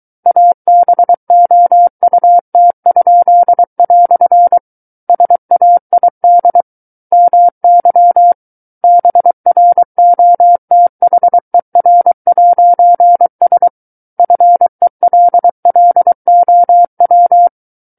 Morse code, heard as a keyed tone, about 23 wpm, 700 Hz. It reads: ABOUT?" SAID MY BROTHER'S FELLOW
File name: Book One / 14-Chapter 14 - 23 WPM